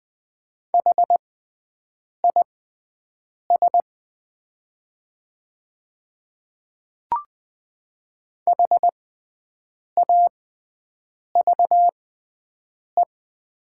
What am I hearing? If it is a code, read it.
Morse code, heard as a keyed tone, about 20 wpm, spaced as for 7 wpm, 700 Hz. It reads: HIS HAVE
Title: 6x Character Spacing - Top 100 Words - 20wpm